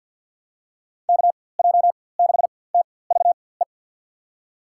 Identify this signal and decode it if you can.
Morse code, heard as a keyed tone, about 52 wpm, spaced as for 24 wpm, 700 Hz. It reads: KJ6TVE
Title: USA Call Signs - 6 Characters - 4x Character Spacing - 50wpm